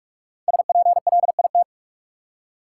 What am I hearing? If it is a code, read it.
Morse code, heard as a keyed tone, about 46 wpm, 700 Hz. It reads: SPLIT
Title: Five Letter Words - 45wpm